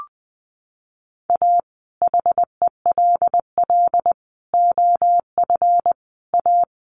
Morse code, heard as a keyed tone, about 20 wpm, 700 Hz. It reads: A HELL OF A
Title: Sets of 5 Words Encore 20wpm